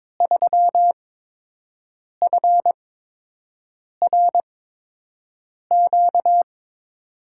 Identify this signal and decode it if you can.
Morse code, heard as a keyed tone, about 22 wpm, spaced as for 6 wpm, 700 Hz. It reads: 3FRQ
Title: USA Call Signs - 5 Characters - 8x Character Spacing - 22wpm